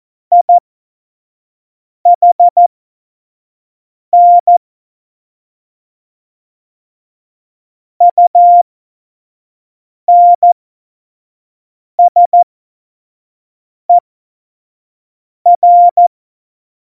Morse code, heard as a keyed tone, about 14 wpm, spaced as for 5 wpm, 700 Hz. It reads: IHN UNSER
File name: Morsekurs / Witze / Witz 27 Juni 17, Tempo 25 F70